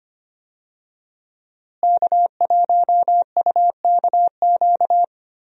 Morse code, heard as a keyed tone, about 25 wpm, 700 Hz. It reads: K1UKQ
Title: USA Call Signs - 5 Characters - Repeated 2x - 25wpm